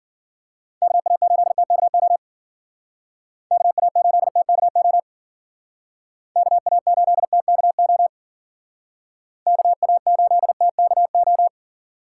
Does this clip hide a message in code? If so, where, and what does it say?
Yes: Morse code from 0.8 to 11.5 s, heard as a keyed tone, about 52 wpm, 700 Hz. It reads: KA8TKO KA8TKO KA8TKO KA8TKO